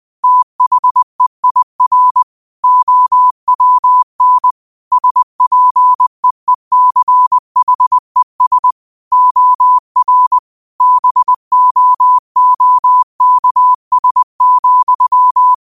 Morse code, heard as a keyed tone, about 20 wpm, 1000 Hz. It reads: THEIR OWN SPEECHES OR BOOKS,
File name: qotdfiles / 20wpm